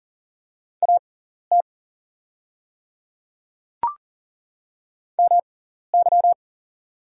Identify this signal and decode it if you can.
Morse code, heard as a keyed tone, about 40 wpm, spaced as for 14 wpm, 690 Hz. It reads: AT MY
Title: 6x Character Spacing - Top 100 Words - 40wpm